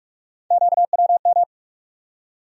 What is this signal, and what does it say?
Morse code, heard as a keyed tone, about 47 wpm, 700 Hz. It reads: QWM
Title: USA Call Sign Postfixes - 3 Letters 45wpm